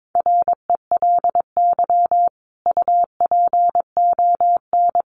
Morse code, heard as a keyed tone, about 22 wpm, 700 Hz. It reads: RELY UPON
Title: Sentences from Top 1200 Words 22wpm